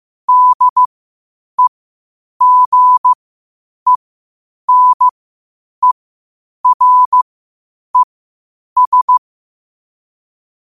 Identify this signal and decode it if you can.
Morse code, heard as a keyed tone, about 15 wpm, spaced as for 8 wpm, 1000 Hz. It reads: DEGENERES
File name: qotdfiles / 8wpm